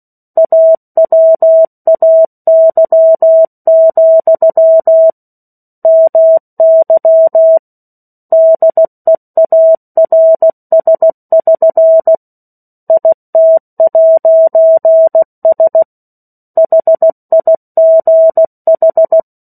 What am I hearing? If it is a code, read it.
Morse code, heard as a keyed tone, about 16 wpm, 650 Hz. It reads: AWAY, MY DEARS<SN> IT'S HIGH
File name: 03-Chapter-03 16 WPM